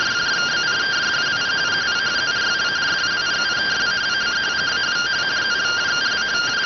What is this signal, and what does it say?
Two-tone teleprinter signal, about 170 Hz shift.